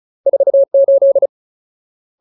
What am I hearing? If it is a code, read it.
Morse code, heard as a keyed tone, about 34 wpm, 550 Hz. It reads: 48
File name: Ultimate Word List / 35WPM Ultimate